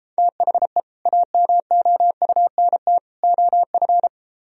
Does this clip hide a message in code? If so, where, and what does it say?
Yes: Morse code from 0.2 to 4.1 s, heard as a keyed tone, about 33 wpm, 700 Hz. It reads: THE AMOUNT OF